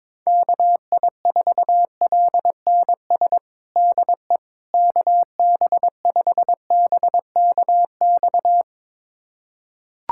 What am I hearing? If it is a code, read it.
Morse code, heard as a keyed tone, about 22 wpm, 700 Hz. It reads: KI4LNS DE KB5BKX
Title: USA Callsign DE Callsign 22wpm